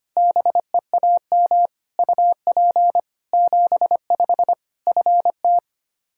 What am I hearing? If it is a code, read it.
Morse code, heard as a keyed tone, about 25 wpm, 700 Hz. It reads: BEAM UP 75 FT
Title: K1USN SST QSO Practice 25wpm